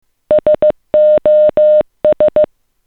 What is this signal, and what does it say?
Morse code, heard as a keyed tone, about 15 wpm, 610 Hz. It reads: SOS
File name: s59dap / sos